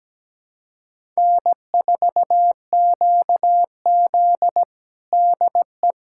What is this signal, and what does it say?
Morse code, heard as a keyed tone, about 17 wpm, 700 Hz. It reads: N4QZ DE
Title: USA Callsign DE Callsign - 4 Characters - 17wpm